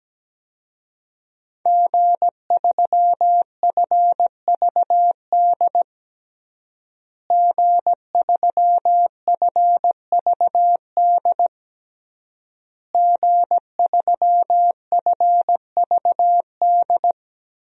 Morse code, heard as a keyed tone, about 17 wpm, 700 Hz. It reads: G3FVD G3FVD G3FVD
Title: DXCC-100 Call Signs - 5 Characters - Repeated 3x - 17wpm